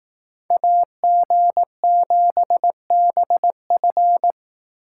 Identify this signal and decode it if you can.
Morse code, heard as a keyed tone, about 18 wpm, 700 Hz. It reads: AG7BF
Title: Speed Racing - USA 2x2 Call Signs 15wpm